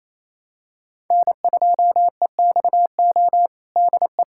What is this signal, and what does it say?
Morse code, heard as a keyed tone, about 28 wpm, 700 Hz. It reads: N2EXO DE
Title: USA Callsign DE Callsign - 5 Characters - 28wpm